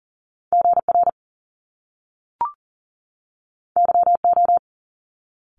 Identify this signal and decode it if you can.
Morse code, heard as a keyed tone, about 38 wpm, 700 Hz. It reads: GR YO
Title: Top Two Letter Combinations in Words 40wpm